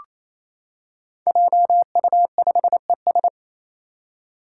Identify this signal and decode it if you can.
Morse code, heard as a keyed tone, about 28 wpm, 700 Hz. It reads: JU5ES
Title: Five Challenging Letter-Number Combinations 28wpm